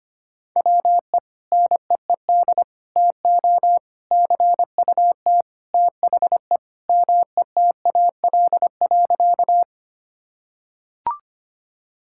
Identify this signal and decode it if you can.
Morse code, heard as a keyed tone, about 25 wpm, 700 Hz. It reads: WE NEED TO CUT THE METAL.
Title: Sentences from Top 1000 Words - Review Part 6 - 25wpm